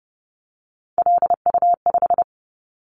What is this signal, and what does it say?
Morse code, heard as a keyed tone, about 30 wpm, 700 Hz. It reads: LU5
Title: Callsign Prefixes for Top 50 DXCC - 30wpm